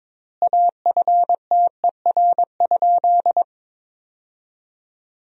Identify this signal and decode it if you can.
Morse code, heard as a keyed tone, about 22 wpm, 700 Hz. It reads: AFTER?
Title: 4x Word-Spacing - Sentences from Top 200 Words - 22wpm